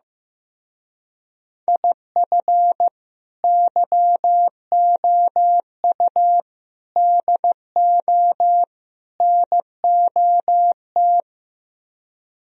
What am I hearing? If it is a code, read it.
Morse code, heard as a keyed tone, about 15 wpm, 700 Hz. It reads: IF YOU DO NOT
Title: Speed-Racing - Sets of 4 Words 15wpm